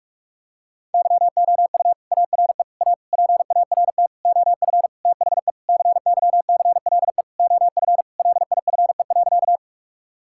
Morse code, heard as a keyed tone, about 43 wpm, 700 Hz. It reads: YOU ARE A PART OF THE CYCLE OF LIFE.
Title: Sentences from Top 1000 Words - Review Part 6 - 45wpm